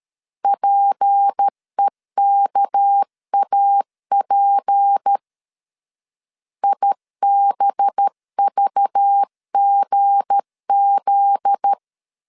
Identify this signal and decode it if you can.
Morse code, heard as a keyed tone, about 13 wpm, 800 Hz. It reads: PEKAP IBVGZ
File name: LEZIONI / 7 GF9Z / 7 GF9Z 12wpm 2147